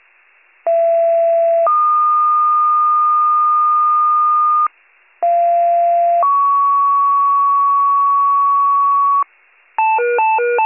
Two-tone page, heard as a one-second tone, then a three-second tone, about 670 then 1220 Hz, about 690 then 1120 Hz. It is heard from 0.7 to 9.2 s.